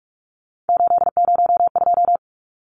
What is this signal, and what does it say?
Morse code, heard as a keyed tone, about 44 wpm, 700 Hz. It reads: 802
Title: Numbers - 3-Digit 45wpm